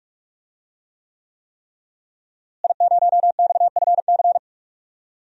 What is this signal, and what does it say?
Morse code, heard as a keyed tone, about 41 wpm, 700 Hz. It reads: I0XFC
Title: DXCC-100 Call Signs - 5 Characters - Repeated 3x - 45wpm